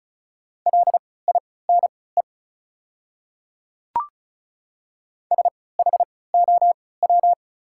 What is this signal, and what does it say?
Morse code, heard as a keyed tone, about 34 wpm, spaced as for 20 wpm, 700 Hz. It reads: LINE SHOW
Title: 3x Character Spacing - Top 200 Words - 35wpm